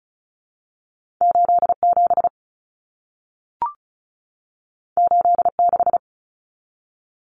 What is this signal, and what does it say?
Morse code, heard as a keyed tone, about 36 wpm, 700 Hz. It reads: 87 86
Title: Common Cut Numbers - 2-Digit 35wpm